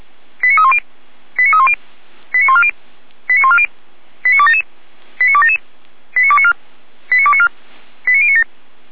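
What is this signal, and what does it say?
Five-tone selective call (ZVEI), heard as quick runs of five tones, tones of about 70 ms. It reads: 89310 89310 89140 89140 89277 89277 89294 89294 89097
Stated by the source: ZVEI five-tone call received by SDR, from the Feuerwehr Orlamünde audio server.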